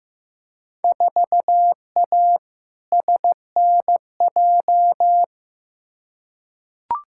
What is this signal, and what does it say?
Morse code, heard as a keyed tone, about 15 wpm, 700 Hz. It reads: 4A SNJ
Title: Field Day Exchanges Encore - 15wpm